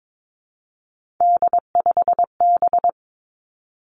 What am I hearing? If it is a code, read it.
Morse code, heard as a keyed tone, about 22 wpm, 700 Hz. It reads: D5B
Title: Three Challenging Letter-Number Combinations 22wpm